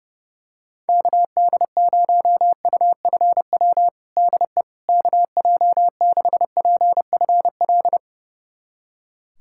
Morse code, heard as a keyed tone, about 30 wpm, 700 Hz. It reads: KD0UFW DE KJ6PFL
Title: USA Callsign DE Callsign - 6 Characters - 30wpm